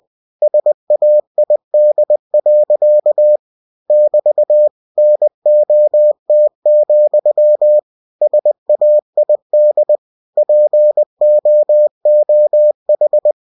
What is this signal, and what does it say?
Morse code, heard as a keyed tone, about 20 wpm, 600 Hz. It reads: SAID. <BT> NOT, SAID POOH
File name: cw-books / winnie-the-pooh / winnie-20wpm-page-0006